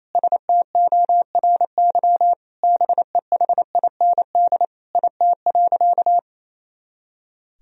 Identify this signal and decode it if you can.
Morse code, heard as a keyed tone, about 28 wpm, 700 Hz. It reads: STORY BEHIND IT.
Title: Sentences from Top 500 Words and Abbreviations 28wpm